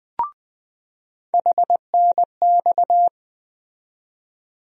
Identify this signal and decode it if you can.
Morse code, heard as a keyed tone, about 20 wpm, 700 Hz. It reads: HNX
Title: USA Call Sign Postfixes - 3 Letters 20wpm